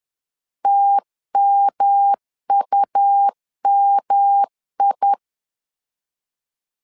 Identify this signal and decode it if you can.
Morse code, heard as a keyed tone, about 11 wpm, 800 Hz. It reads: TMUMI
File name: LEZIONI / 3 MWRO / 3 M 10wpm 1828